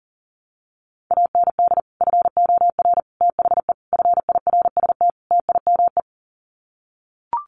Morse code, heard as a keyed tone, about 40 wpm, 700 Hz. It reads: AND FOR THE FIRST TIME
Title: Sets of 5 Words 40wpm